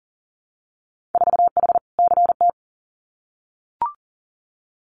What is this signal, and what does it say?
Morse code, heard as a keyed tone, about 42 wpm, 700 Hz. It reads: 4H CT
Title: Winter Field Day Exchanges Encore 40wpm